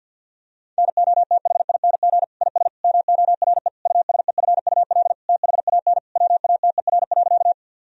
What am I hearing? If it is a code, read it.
Morse code, heard as a keyed tone, about 50 wpm, 700 Hz. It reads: NOTHING IS MORE USEFUL THAN WATER.
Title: Sentences from Top 1000 Words - Review Part 4 - 50wpm